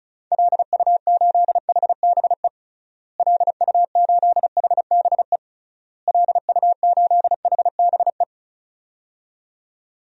Morse code, heard as a keyed tone, about 36 wpm, 700 Hz. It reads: LU8HBE LU8HBE LU8HBE
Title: DXCC-100 Call Signs - 6 Characters - Repeated 3x - 35wpm